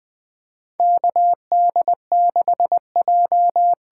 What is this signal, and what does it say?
Morse code, heard as a keyed tone, about 20 wpm, 700 Hz. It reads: KD6J
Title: USA Call Signs - 6 Characters - 8x Character Spacing - 20wpm